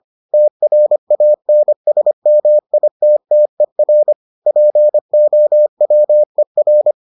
Morse code, heard as a keyed tone, about 25 wpm, 600 Hz. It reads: TRANSMITTER POWER